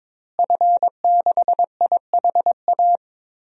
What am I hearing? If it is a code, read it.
Morse code, heard as a keyed tone, about 22 wpm, 700 Hz. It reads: F6IHA
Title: DXCC-100 Call Signs - 5 Characters - Repeated 3x - 22wpm